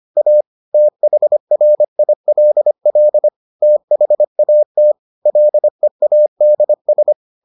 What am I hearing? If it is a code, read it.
Morse code, heard as a keyed tone, about 25 wpm, 600 Hz. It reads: A THRILL THAT LEADS